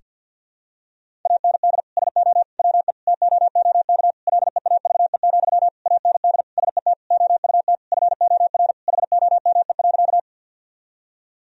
Morse code, heard as a keyed tone, about 52 wpm, 700 Hz. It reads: AND SO WE TOOK LEAVE, AND SET OUT FOR HOME.
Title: Sentences from Top 1000 Words - Review Part 1 - 50wpm